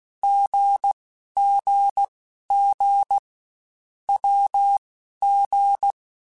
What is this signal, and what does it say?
Morse code, heard as a keyed tone, about 16 wpm, spaced as for 11 wpm, 800 Hz. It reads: GGG WG